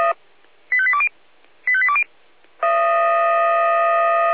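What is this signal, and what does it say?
Five-tone selective call (ZVEI), heard as quick runs of five tones, tones of about 70 ms. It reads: 85820 85820
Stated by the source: ZVEI five-tone selective call received on SDR.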